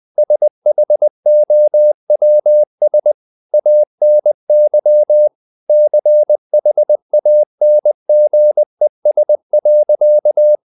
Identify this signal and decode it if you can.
Morse code, heard as a keyed tone, about 20 wpm, 600 Hz. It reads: SHOWS ANY CHANGES.
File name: the old man and the sea part 2 20wpm